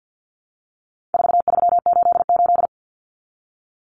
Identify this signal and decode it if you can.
Morse code, heard as a keyed tone, about 49 wpm, 700 Hz. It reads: 4388
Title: Common Cut Numbers - 4-Digit 50wpm